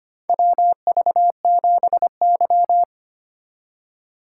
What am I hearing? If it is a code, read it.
Morse code, heard as a keyed tone, about 25 wpm, 700 Hz. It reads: WV7Y
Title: USA Call Signs - 4x Character Spacing - 25wpm